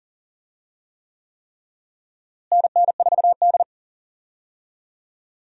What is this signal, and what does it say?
Morse code, heard as a keyed tone, about 40 wpm, 700 Hz. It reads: NN4D
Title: USA Call Signs - 4 Characters - Repeated 3x - 40wpm